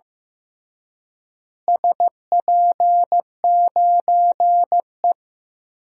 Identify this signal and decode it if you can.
Morse code, heard as a keyed tone, about 16 wpm, 700 Hz. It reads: SP9E
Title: DXCC-100 Call Signs - 4 Characters - Repeated 3x - 15wpm